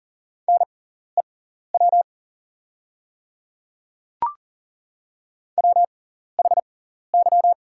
Morse code, heard as a keyed tone, about 40 wpm, spaced as for 14 wpm, 690 Hz. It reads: NEW WHY
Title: 6x Character Spacing - Top 300 Words - 40wpm